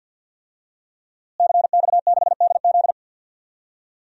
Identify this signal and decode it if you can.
Morse code, heard as a keyed tone, about 46 wpm, 700 Hz. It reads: CX6DZ
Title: DXCC-100 Call Signs - 5 Characters - Repeated 3x - 50wpm